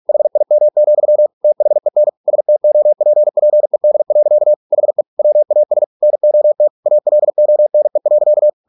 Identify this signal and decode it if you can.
Morse code, heard as a keyed tone, about 46 wpm, 600 Hz. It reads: HIM, THEN STOPPED. HE WAS NOT ALONE.